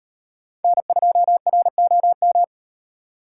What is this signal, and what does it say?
Morse code, heard as a keyed tone, about 38 wpm, 700 Hz. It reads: N2ROM